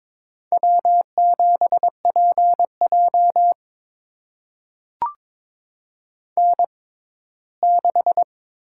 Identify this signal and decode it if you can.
Morse code, heard as a keyed tone, about 22 wpm, spaced as for 15 wpm, 700 Hz. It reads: W7PJ N6
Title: USA Call Signs - 4 Characters - 6x Character Spacing - 22wpm